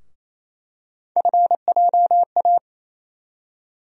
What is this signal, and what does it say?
Morse code, heard as a keyed tone, about 27 wpm, 700 Hz. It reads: FJA
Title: USA Call Sign Postfixes - 3 Letters 28wpm